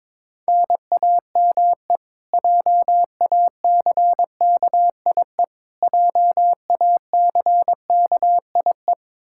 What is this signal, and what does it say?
Morse code, heard as a keyed tone, about 22 wpm, 700 Hz. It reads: NAME JACKIE JACKIE
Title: Practice QSO - Exchange 1 - Beginning - 22wpm